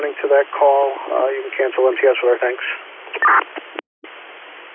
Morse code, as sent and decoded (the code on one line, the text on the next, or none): none